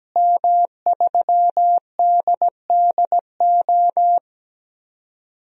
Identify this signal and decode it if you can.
Morse code, heard as a keyed tone, about 17 wpm, 700 Hz. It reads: M3DDO
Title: DXCC-100 Call Signs - 5 Characters - Repeated 3x - 17wpm